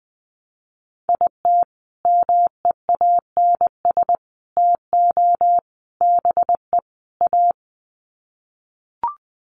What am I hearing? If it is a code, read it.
Morse code, heard as a keyed tone, about 20 wpm, 700 Hz. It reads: IT MEANS TO BE A
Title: Sets of 5 Words Encore2 20wpm